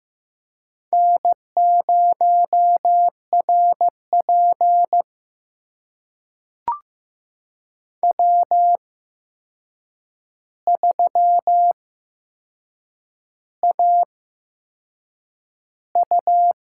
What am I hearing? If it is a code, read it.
Morse code, heard as a keyed tone, about 15 wpm, spaced as for 6 wpm, 700 Hz. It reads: N0RP W3AU
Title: USA Call Signs - 4 Characters - 8x Character Spacing - 15wpm